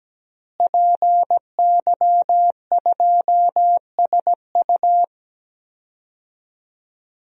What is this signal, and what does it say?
Morse code, heard as a keyed tone, about 17 wpm, 700 Hz. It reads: PY2SU
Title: DXCC-100 Call Signs - 5 Characters - Repeated 2x - 17wpm